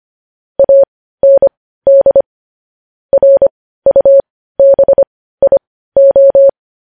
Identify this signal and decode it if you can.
Morse code, heard as a keyed tone, about 25 wpm, spaced as for 15 wpm, 550 Hz. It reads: AND RUBIO
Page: News Headlines in Morse Code at 15 WPM